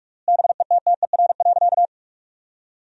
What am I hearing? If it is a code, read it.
Morse code, heard as a keyed tone, about 48 wpm, 700 Hz. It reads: BETTER.
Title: 3x Word-Spacing - Sentences from Top 500 Words - 45wpm